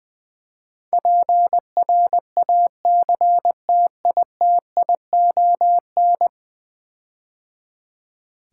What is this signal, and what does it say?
Morse code, heard as a keyed tone, about 20 wpm, 700 Hz. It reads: PRACTITION
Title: Build A Word - 5 Letters At A Time - 12-Letter Words - 20wpm